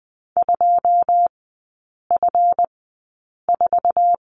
Morse code, heard as a keyed tone, about 20 wpm, spaced as for 8 wpm, 690 Hz. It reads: 2F4
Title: 145 Lesson 24e - Letter y - Three Characters 20wpm